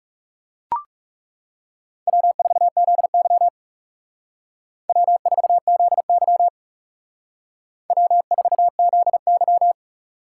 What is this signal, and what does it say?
Morse code, heard as a keyed tone, about 38 wpm, 700 Hz. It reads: W4ZY W4ZY W4ZY